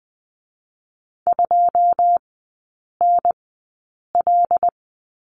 Morse code, heard as a keyed tone, about 20 wpm, spaced as for 8 wpm, 700 Hz. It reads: 2NL
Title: 127 Lesson 21e - Number 6 - Three Characters 20wpm